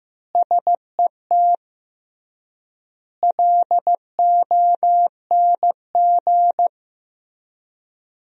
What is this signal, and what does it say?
Morse code, heard as a keyed tone, about 15 wpm, 700 Hz. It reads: SET LONG
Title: 3x Word-Spacing - Sentences from Top 300 Words - 15wpm